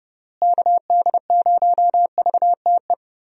Morse code, heard as a keyed tone, about 30 wpm, 700 Hz. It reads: KD0VTE
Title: USA Call Signs - 6 Characters - Repeated 3x - 30wpm